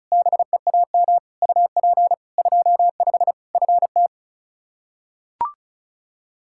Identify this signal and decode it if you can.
Morse code, heard as a keyed tone, about 36 wpm, 700 Hz. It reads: BEAM UP 25 FT
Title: Practice QSO - Exchange 2 - Info - 35wpm